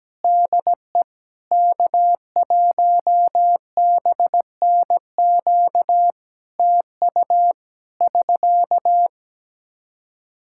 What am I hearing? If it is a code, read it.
Morse code, heard as a keyed tone, about 17 wpm, 700 Hz. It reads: DE K1BNQ TU <SK>
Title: Practice QSO - Exchange 4 - Ending - 17wpm